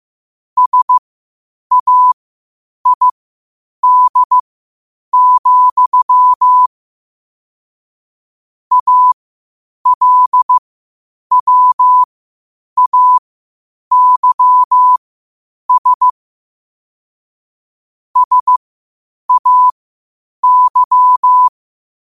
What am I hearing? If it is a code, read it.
Morse code, heard as a keyed tone, about 15 wpm, spaced as for 8 wpm, 1000 Hz. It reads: SAID, ALWAYS SAY
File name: qotdfiles / 8wpm